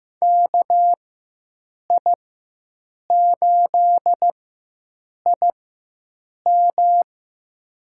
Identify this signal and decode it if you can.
Morse code, heard as a keyed tone, about 15 wpm, spaced as for 7 wpm, 700 Hz. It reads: KI8IM
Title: USA Call Signs - 5 Characters - 4x Character Spacing - 15wpm